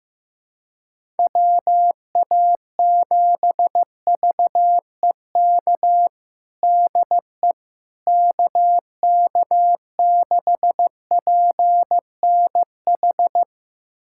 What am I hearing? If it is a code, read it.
Morse code, heard as a keyed tone, about 15 wpm, 700 Hz. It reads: WA7VEK DE KK6PNH
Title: USA Callsign DE Callsign - 6 Characters - 15wpm